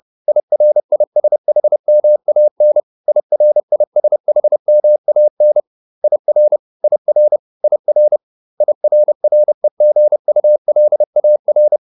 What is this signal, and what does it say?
Morse code, heard as a keyed tone, about 30 wpm, 600 Hz. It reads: IRISHMAN IRISHMAN IR IR IR IRREGULAR